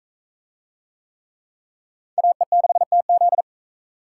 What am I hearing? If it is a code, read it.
Morse code, heard as a keyed tone, about 40 wpm, 700 Hz. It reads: AE6TZ